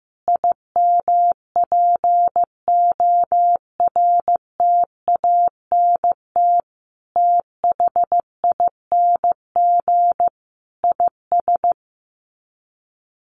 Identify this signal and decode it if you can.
Morse code, heard as a keyed tone, about 14 wpm, 700 Hz. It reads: IMPORTANT THING IS
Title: Sets of 5 Words Encore 15wpm